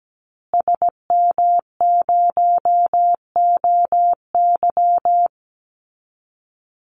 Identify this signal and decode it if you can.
Morse code, heard as a keyed tone, about 17 wpm, 700 Hz. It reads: SM0OY
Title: Call Signs Repeated 2x - 17wpm